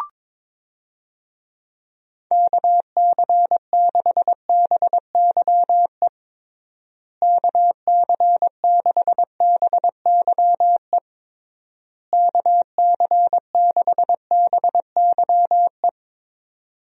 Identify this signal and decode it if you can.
Morse code, heard as a keyed tone, about 22 wpm, 700 Hz. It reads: KC6BYE KC6BYE KC6BYE
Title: USA Call Signs - 6 Characters - Repeated 3x - 22wpm